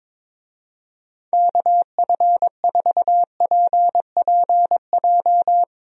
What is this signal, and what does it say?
Morse code, heard as a keyed tone, about 22 wpm, 700 Hz. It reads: KF4PPJ
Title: USA Call Signs - 6 Characters - 22wpm